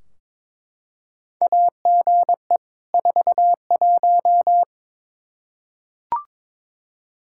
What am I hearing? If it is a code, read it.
Morse code, heard as a keyed tone, about 22 wpm, 700 Hz. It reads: AGE 41
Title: Practice QSO - Exchange 3 - More Info - 22wpm